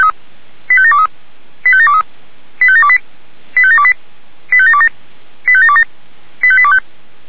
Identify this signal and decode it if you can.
Five-tone selective call (ZVEI), heard as quick runs of five tones, tones of about 70 ms. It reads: 85723 85723 85728 85728 85727 85727 85825
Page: SDR/ZVEI